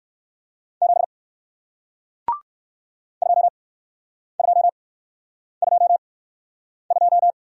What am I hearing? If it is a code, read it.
Morse code, heard as a keyed tone, about 57 wpm, spaced as for 9 wpm, 700 Hz. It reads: 7 2222